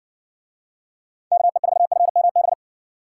Morse code, heard as a keyed tone, about 60 wpm, 700 Hz. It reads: KE4FMB